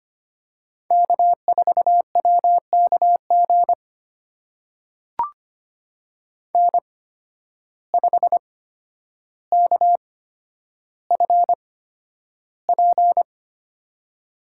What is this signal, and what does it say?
Morse code, heard as a keyed tone, about 25 wpm, spaced as for 11 wpm, 700 Hz. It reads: K4WKG N5KFP
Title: USA Call Signs - 5 Characters - 8x Character Spacing - 25wpm